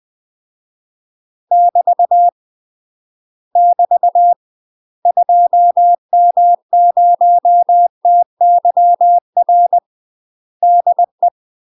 Morse code, heard as a keyed tone, about 20 wpm, 700 Hz. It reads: <BT> <BT> 2M0TYR DE
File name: UK MASTER QSOs TRAINING / 20WPM / UK MASTER QSOs 0070